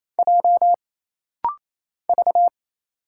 Morse code, heard as a keyed tone, about 27 wpm, 700 Hz. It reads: J V
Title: Single Letters - Rapid-Fire 28wpm